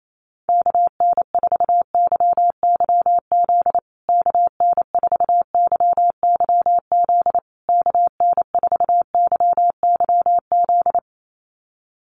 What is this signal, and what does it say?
Morse code, heard as a keyed tone, about 28 wpm, 700 Hz. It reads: KN4YYZ KN4YYZ KN4YYZ
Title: USA Call Signs Repeated 3x 28wpm